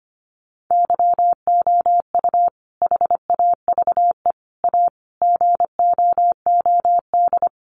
Morse code, heard as a keyed tone, about 25 wpm, 700 Hz. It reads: YOU HAVE A GOOD
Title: Sentences from Top 1300 Words 25wpm